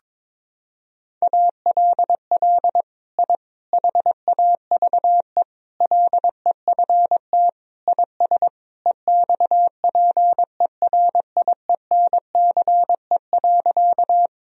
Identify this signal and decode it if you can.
Morse code, heard as a keyed tone, about 22 wpm, 700 Hz. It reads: ALL I HAVE LEFT IS EXPERIENCE.
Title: Sentences from Top 1000 Words - Review Part 3 - 22wpm